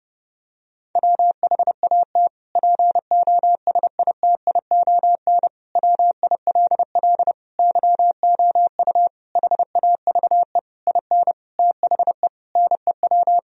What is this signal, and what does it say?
Morse code, heard as a keyed tone, about 30 wpm, 700 Hz. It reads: WHAT POSITION WILL YOU HAVE IN THE NEW
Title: Sentences from Top 1000 Words - Review Part 3 - 30wpm